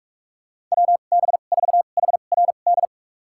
Morse code, heard as a keyed tone, about 46 wpm, spaced as for 33 wpm, 700 Hz. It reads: WB4HRD
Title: USA Call Signs - 2x Character Spacing - 45wpm